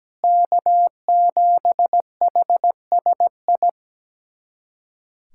Morse code, heard as a keyed tone, about 17 wpm, 700 Hz. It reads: K7HSI
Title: USA Callsign DE Callsign 17wpm